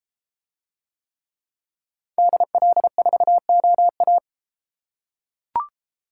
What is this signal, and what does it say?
Morse code, heard as a keyed tone, about 32 wpm, 700 Hz. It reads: DL4OA